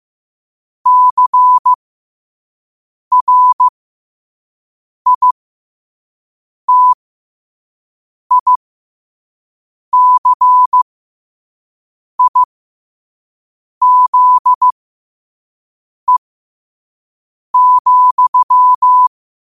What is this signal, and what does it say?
Morse code, heard as a keyed tone, about 15 wpm, spaced as for 5 wpm, 1000 Hz. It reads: CRITICIZE,
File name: qotdfiles / 5wpm